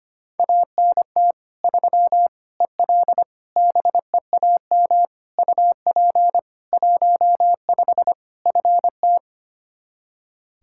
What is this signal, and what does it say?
Morse code, heard as a keyed tone, about 25 wpm, 700 Hz. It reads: ANT 3 EL BEAM UP 15 FT
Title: Practice QSO - Exchange 2 - Info - 25wpm